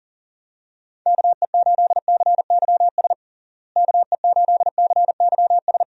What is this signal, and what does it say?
Morse code, heard as a keyed tone, about 37 wpm, 700 Hz. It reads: KE8CYS KE8CYS
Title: USA Call Signs - 6 Characters - Repeated 2x - 40wpm